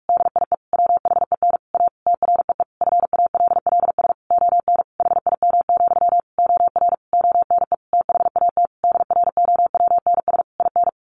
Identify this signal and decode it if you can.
Morse code, heard as a keyed tone, about 44 wpm, 700 Hz. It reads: DIE WHEN A TREE FALLS ON HIM, OR ONE THAT DROWNS IN
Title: A Sound Of Thunder 45wpm